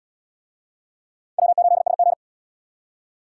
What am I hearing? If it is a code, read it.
Morse code, heard as a keyed tone, about 66 wpm, 700 Hz. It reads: W8SG